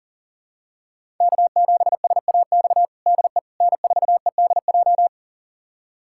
Morse code, heard as a keyed tone, about 40 wpm, 700 Hz. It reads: K7SAX DE N4EDJ
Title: USA Callsign DE Callsign - 5 Characters - 40wpm